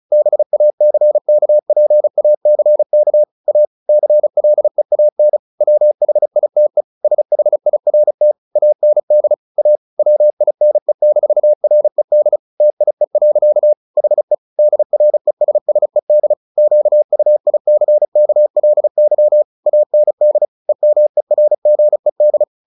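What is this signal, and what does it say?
Morse code, heard as a keyed tone, about 34 wpm, 600 Hz. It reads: BACKPACK A CLEAN WHITE SHIRT AND A WINE-RED TIE. HE DRESSED QUICKLY AND EMERGED
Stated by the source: Morse practice at 35 wpm from CW med Gnister, sending Ken Follett's novel Never in English.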